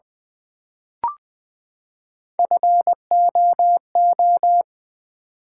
Morse code, heard as a keyed tone, about 20 wpm, 700 Hz. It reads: FOO
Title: Build A Word - 3 Letters At A Time - 8-Letter Words - 20wpm